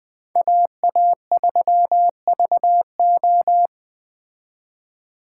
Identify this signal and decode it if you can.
Morse code, heard as a keyed tone, about 20 wpm, 700 Hz. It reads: AA3VO
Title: Speed Racing - USA 2x2 Call Signs 17wpm